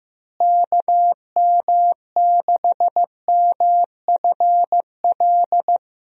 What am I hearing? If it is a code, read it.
Morse code, heard as a keyed tone, about 15 wpm, 700 Hz. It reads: KM6MFL
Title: USA Call Signs - 6 Characters - Repeated 2x - 15wpm